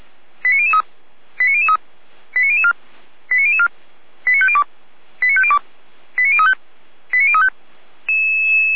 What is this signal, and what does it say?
Five-tone selective call (ZVEI), heard as quick runs of five tones, tones of about 70 ms. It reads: 89003 89003 89004 89004 89502 89502 89035 89035
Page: audio.f85360.feuerwehr-orlamuende.org